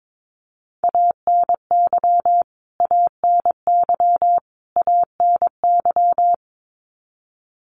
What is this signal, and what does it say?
Morse code, heard as a keyed tone, about 22 wpm, 700 Hz. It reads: ANY ANY ANY
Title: Top 300 Words Repeated 22wpm